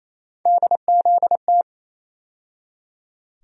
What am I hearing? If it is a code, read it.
Morse code, heard as a keyed tone, about 28 wpm, 700 Hz. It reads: DZT
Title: USA Call Sign Postfixes - 3 Letters 28wpm